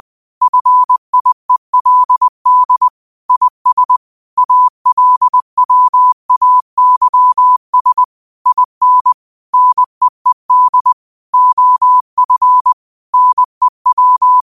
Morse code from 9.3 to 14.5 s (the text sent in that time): NEED OF NEW